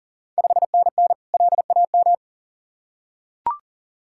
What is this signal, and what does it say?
Morse code, heard as a keyed tone, about 40 wpm, 700 Hz. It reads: 5NN LAM
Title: Ontario QSO Party Exchange 40wpm